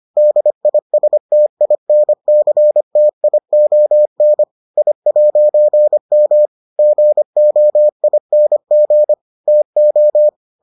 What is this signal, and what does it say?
Morse code, heard as a keyed tone, about 25 wpm, 600 Hz. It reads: DISTINCTION I'M GOING TO